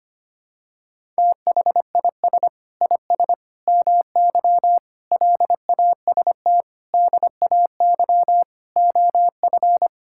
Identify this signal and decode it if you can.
Morse code, heard as a keyed tone, about 25 wpm, 700 Hz. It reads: THIS IS MY LAST DAY OF
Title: Sentences from Top 1000 Words - Review Part 5 - 25wpm